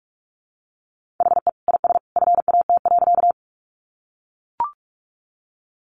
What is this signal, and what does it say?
Morse code, heard as a keyed tone, about 44 wpm, 700 Hz. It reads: HE IS FAT.
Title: Easy Phrases 45wpm